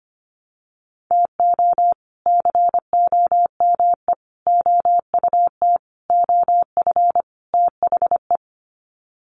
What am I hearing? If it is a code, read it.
Morse code, heard as a keyed tone, about 25 wpm, 700 Hz. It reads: TO COME OUT OF THE
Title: Sets of 5 Words Encore2 25wpm